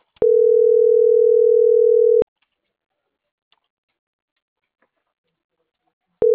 Ringback tone (North America), 440 + 480 Hz.